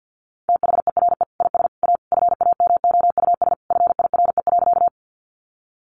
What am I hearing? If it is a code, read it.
Morse code, heard as a keyed tone, about 50 wpm, 700 Hz. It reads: THERE IS A FAMOUS FIRE.
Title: Sentences from Top 1500 Words 50wpm